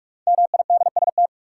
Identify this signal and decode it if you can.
Morse code, heard as a keyed tone, about 46 wpm, 700 Hz. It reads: MIDST
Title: Sets of 4 Words - New Word Drill - 45wpm